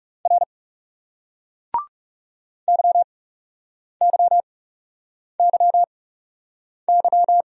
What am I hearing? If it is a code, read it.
Morse code, heard as a keyed tone, about 39 wpm, spaced as for 8 wpm, 700 Hz. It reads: R YYYY